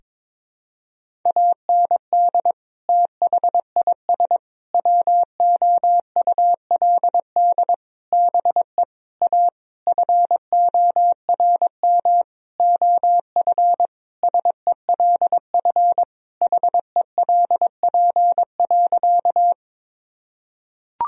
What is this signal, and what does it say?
Morse code, heard as a keyed tone, about 22 wpm, 700 Hz. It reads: AND THIS WOULD BE A FORM OF SELF HELP.
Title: Sentences from Top 1000 Words - Review Part 1 - 22wpm